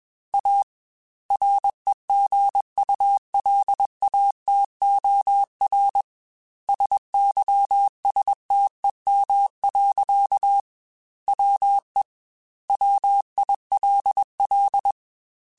Morse code, heard as a keyed tone, about 21 wpm, 800 Hz. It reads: A REGULATOR SYSTEM. WE WILL
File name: Technical Subjects / Technical Subjects-mp3s / Sw Pwr Sup @20